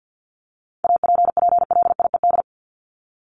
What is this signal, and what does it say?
Morse code, heard as a keyed tone, about 50 wpm, 700 Hz. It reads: APPLIED
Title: ED Suffix - Top 100 Words with Base Words 50wpm